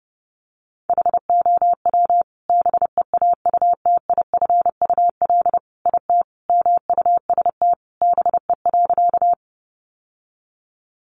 Morse code, heard as a keyed tone, about 28 wpm, 700 Hz. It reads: HOW BEAUTIFUL IT MUST BE.
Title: Sentences from Top 900 Words 30wpm